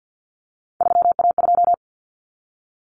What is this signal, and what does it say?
Morse code, heard as a keyed tone, about 50 wpm, 700 Hz. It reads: 3A2
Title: Common Cut Numbers - 3-Digit 50wpm